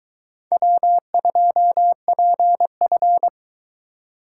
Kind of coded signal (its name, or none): Morse code